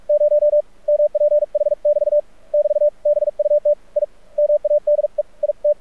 Morse code, heard as a keyed tone, about 45 wpm, 590 Hz. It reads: 0 MPH <BT> <BT> BUT I MADE IT
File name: archived / CRSnet